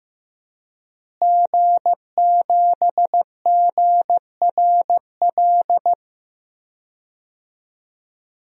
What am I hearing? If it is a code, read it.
Morse code, heard as a keyed tone, about 15 wpm, 700 Hz. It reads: G7GRL
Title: DXCC-100 Call Signs - 5 Characters - Repeated 2x - 15wpm